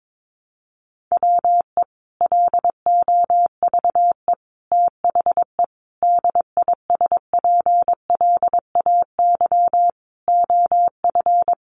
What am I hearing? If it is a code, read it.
Morse code, heard as a keyed tone, about 22 wpm, 700 Hz. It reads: WE LOVE THE DISPLAY OF
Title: Sentences from Top 1200 Words 22wpm